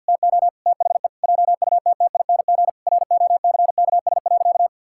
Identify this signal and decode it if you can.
Morse code, heard as a keyed tone, about 50 wpm, 700 Hz. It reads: TO THE JUTTING ROCKS.
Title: Pangram Challenge v2 - 50wpm